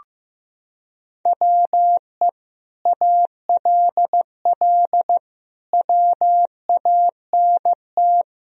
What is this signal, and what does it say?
Morse code, heard as a keyed tone, about 15 wpm, 700 Hz. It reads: WE ALL WANT
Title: Sentences from Top 1000 Words - Review Part 4 - 15wpm